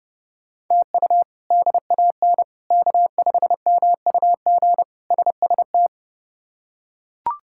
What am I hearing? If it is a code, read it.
Morse code, heard as a keyed tone, about 30 wpm, 700 Hz. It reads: TU DAN K5MUG SST
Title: K1USN SST QSO Practice - 30wpm